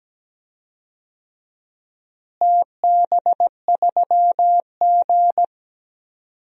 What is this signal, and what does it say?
Morse code, heard as a keyed tone, about 17 wpm, 700 Hz. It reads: TB3G
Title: DXCC-100 Call Signs - 4 Characters - Repeated 2x - 17wpm